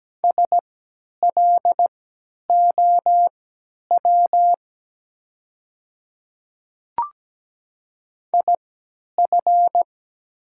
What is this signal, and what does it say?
Morse code, heard as a keyed tone, about 17 wpm, spaced as for 10 wpm, 700 Hz. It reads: SLOW IF
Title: 3x Character Spacing - Top 500 Words - 17wpm